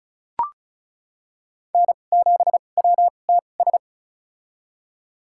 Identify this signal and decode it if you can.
Morse code, heard as a keyed tone, about 36 wpm, spaced as for 25 wpm, 700 Hz. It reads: N7WTS
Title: USA Call Signs - 2x Character Spacing - 35wpm